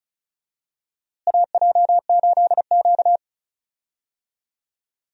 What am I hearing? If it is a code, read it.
Morse code, heard as a keyed tone, about 34 wpm, 700 Hz. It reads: AJ8Q
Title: USA Call Signs - 4 Characters - Repeated 3x - 35wpm